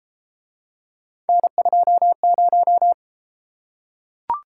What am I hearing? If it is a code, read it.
Morse code, heard as a keyed tone, about 33 wpm, 700 Hz. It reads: N20